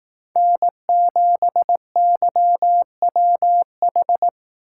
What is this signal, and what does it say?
Morse code, heard as a keyed tone, about 19 wpm, 700 Hz. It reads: N7YWH
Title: Speed Racing - USA 1x3 Call Signs 15wpm